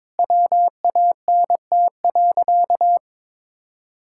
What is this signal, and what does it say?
Morse code, heard as a keyed tone, about 21 wpm, 700 Hz. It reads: WANT.
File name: practice / 22wpm / Sentences with 3-Word Phrases - 22wpm